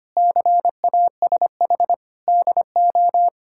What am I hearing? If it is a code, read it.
Morse code, heard as a keyed tone, about 25 wpm, 700 Hz. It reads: CASH DO
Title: Sentences from Top 1000 Words - Review Part 5 - 25wpm